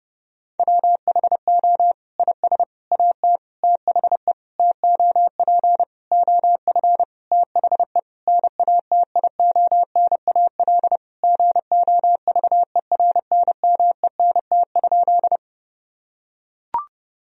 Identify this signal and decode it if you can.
Morse code, heard as a keyed tone, about 32 wpm, 700 Hz. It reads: WHO IS AT THE TOP OF THE NATIONAL GOVERNMENT?
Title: Sentences from Top 1000 Words - Review Part 4 - 30wpm